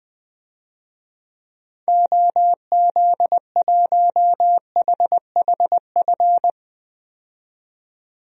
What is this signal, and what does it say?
Morse code, heard as a keyed tone, about 20 wpm, 700 Hz. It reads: OZ1HHF
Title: DXCC-100 Call Signs - 6 Characters - 20wpm